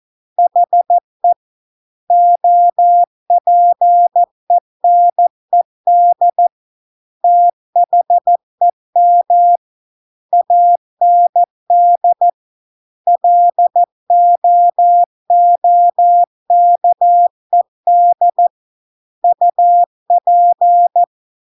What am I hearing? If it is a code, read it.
Morse code, heard as a keyed tone, about 14 wpm, 700 Hz. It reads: HE OPENED THEM AND LOOKED UP